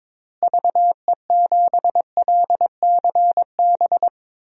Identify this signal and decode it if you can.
Morse code, heard as a keyed tone, about 22 wpm, 700 Hz. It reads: VE7LCB
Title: DXCC-100 Call Signs - 6 Characters - Repeated 3x - 22wpm